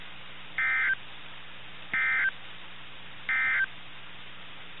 Emergency Alert System (SAME) tones, heard as harsh buzzing bursts. It is heard from 0.5 to 3.7 s.
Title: NOAA Weather Radio - KIH61